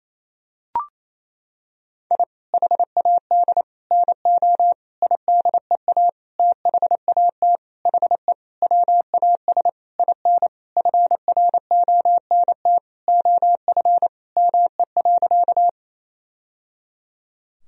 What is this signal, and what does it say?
Morse code, heard as a keyed tone, about 28 wpm, 700 Hz. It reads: I HAD NO IDEA THAT HE WAS IN FRONT OF ME.
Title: Sentences from Top 1000 Words - Review Part 2 - 28wpm